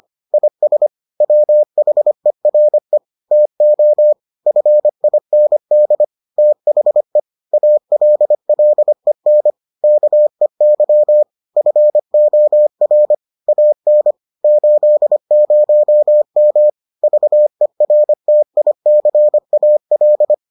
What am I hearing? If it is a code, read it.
Morse code, heard as a keyed tone, about 25 wpm, 600 Hz. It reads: IS WHERE TO FIND THE ALLEN KEY FOR AN 80M VERTICAL